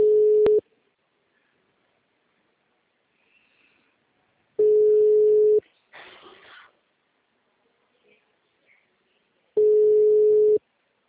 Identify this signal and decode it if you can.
Ringback tone (Europe, 425 Hz).